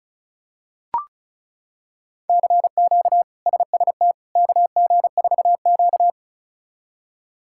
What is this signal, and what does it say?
Morse code, heard as a keyed tone, about 36 wpm, 700 Hz. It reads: CQ SST KG4Q
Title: K1USN SST QSO Practice - 35wpm